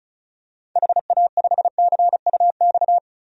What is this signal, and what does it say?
Morse code, heard as a keyed tone, about 34 wpm, 700 Hz. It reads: HA5CUX
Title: DXCC-100 Call Signs - 6 Characters - Repeated 2x - 35wpm